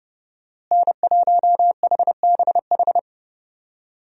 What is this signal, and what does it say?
Morse code, heard as a keyed tone, about 30 wpm, 700 Hz. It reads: N1HBH
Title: USA Call Signs - 5 Characters - Repeated 2x - 30wpm